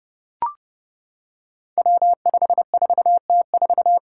Morse code, heard as a keyed tone, about 30 wpm, 700 Hz. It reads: W54T4
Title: Five Challenging Letter-Number Combinations 30wpm